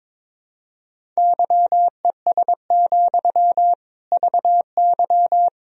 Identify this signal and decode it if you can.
Morse code, heard as a keyed tone, about 22 wpm, 700 Hz. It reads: YES, VY
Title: Sentences from Top 500 Words and Abbreviations 22wpm